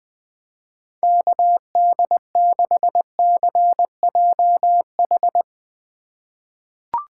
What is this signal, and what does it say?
Morse code, heard as a keyed tone, about 20 wpm, 700 Hz. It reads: KD6CJH
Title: USA Call Signs - 6 Characters - 4x Character Spacing - 20wpm